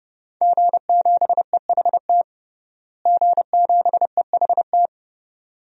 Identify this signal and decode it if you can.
Morse code, heard as a keyed tone, about 30 wpm, 700 Hz. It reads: G7EHT G7EHT
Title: DXCC-100 Call Signs - 5 Characters - Repeated 2x - 30wpm